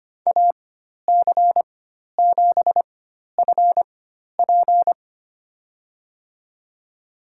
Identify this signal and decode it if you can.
Morse code, heard as a keyed tone, about 25 wpm, spaced as for 12 wpm, 700 Hz. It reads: AC7FP
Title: USA Call Signs - 5 Characters - 4x Character Spacing - 25wpm